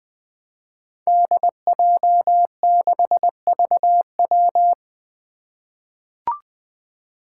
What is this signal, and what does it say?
Morse code, heard as a keyed tone, about 20 wpm, 700 Hz. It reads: DJ6VW
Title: Five Challenging Letter-Number Combinations 20wpm